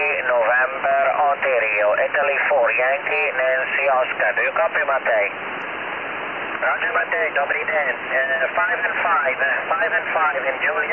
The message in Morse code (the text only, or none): none